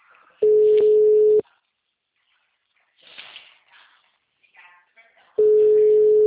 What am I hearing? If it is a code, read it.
Ringback tone (Europe, 425 Hz).
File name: out- -2016- - - .62420